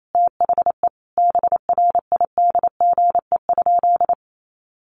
Morse code, heard as a keyed tone, about 28 wpm, 700 Hz. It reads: THE BRIDGE?
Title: Sentences from Top 1200 Words 28wpm